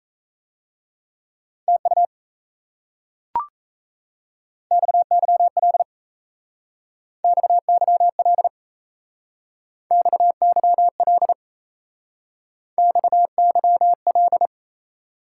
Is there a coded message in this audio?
Yes: Morse code from 1.7 to 14.5 s, heard as a keyed tone, about 36 wpm, 700 Hz. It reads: TU XYL XYL XYL XYL